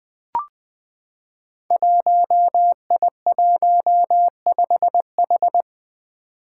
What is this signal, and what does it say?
Morse code, heard as a keyed tone, about 20 wpm, 700 Hz. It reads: 1I15H
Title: Five Challenging Letter-Number Combinations 20wpm